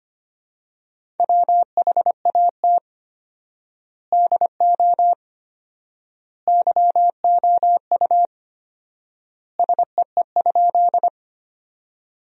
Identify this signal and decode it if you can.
Morse code, heard as a keyed tone, about 25 wpm, 700 Hz. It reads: WHAT DO YOU SEE?
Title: 4x Word-Spacing - Sentences from Top 100 Words - 25wpm